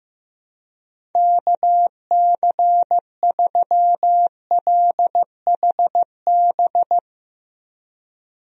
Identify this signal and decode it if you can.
Morse code, heard as a keyed tone, about 15 wpm, 700 Hz. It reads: KC3LHB
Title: USA Call Signs - 6 Characters - Repeated 3x - 15wpm